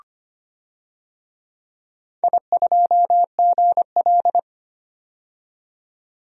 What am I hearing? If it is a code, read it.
Morse code, heard as a keyed tone, about 25 wpm, 700 Hz. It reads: I2GL
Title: DXCC-100 Call Signs - 4 Characters - 25wpm